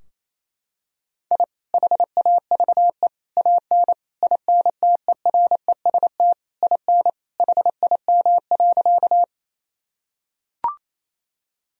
Morse code, heard as a keyed tone, about 28 wpm, 700 Hz. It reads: I HAVE AN INTEREST IN HIM.
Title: Sentences from Top 1000 Words - Review Part 2 - 28wpm